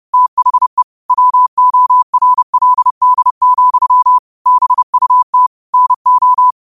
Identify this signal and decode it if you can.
Morse code, heard as a keyed tone, about 30 wpm, 1000 Hz. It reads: THE WORLD, BUT NO
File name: 2026-02-23-30wpm